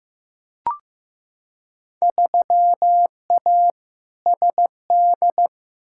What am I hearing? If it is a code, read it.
Morse code, heard as a keyed tone, about 15 wpm, 700 Hz. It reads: 3A SD
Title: Field Day Exchanges - 15wpm